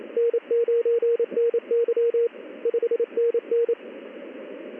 Morse code, heard as a keyed tone, about 28 wpm, 470 Hz, with radio noise. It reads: N9NY 5NN
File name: contests / ss-ssb-k5tr- -1700